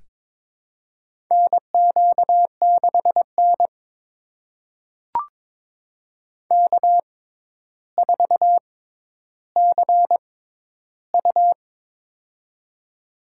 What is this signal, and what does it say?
Morse code, heard as a keyed tone, about 22 wpm, spaced as for 11 wpm, 700 Hz. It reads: NQ6N K4CU
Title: USA Call Signs - 4 Characters - 6x Character Spacing - 22wpm